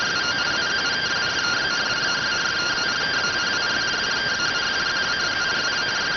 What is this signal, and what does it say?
Two-tone teleprinter signal, about 170 Hz shift.
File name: findings / SAGITTAL 43mm 4mm.gen-06 rx